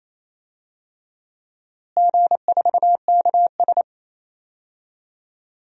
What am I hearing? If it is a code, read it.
Morse code, heard as a keyed tone, about 28 wpm, 700 Hz. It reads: G4KS
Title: DXCC-100 Call Signs - 4 Characters - Repeated 2x - 28wpm